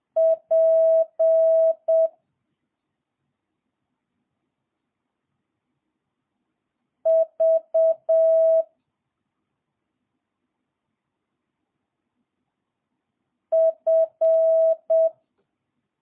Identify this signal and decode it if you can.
Morse code, heard as a keyed tone, about 7 wpm, 650 Hz. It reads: P V F
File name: jh8cbh / 2022cw